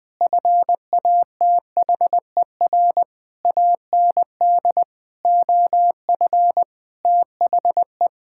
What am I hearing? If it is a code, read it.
Morse code, heard as a keyed tone, about 20 wpm, 700 Hz. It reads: FATHER AND OF THE
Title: Sentences with 9-Word Phrases - 20wpm